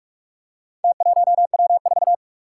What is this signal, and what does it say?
Morse code, heard as a keyed tone, about 44 wpm, 700 Hz. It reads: T1W4
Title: Four Challenging Letter-Number Combinations 45wpm